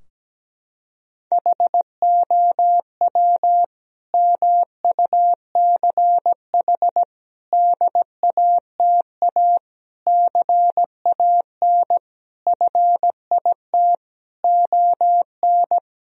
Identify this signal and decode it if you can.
Morse code, heard as a keyed tone, about 17 wpm, 700 Hz. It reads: HOW MUCH DATA CAN FIT ON
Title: Sentences from Top 1000 Words - Review Part 4 - 17wpm